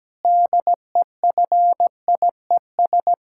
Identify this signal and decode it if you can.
Morse code, heard as a keyed tone, about 17 wpm, 700 Hz. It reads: DEFIES
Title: Pangram Challenge v2 - 17wpm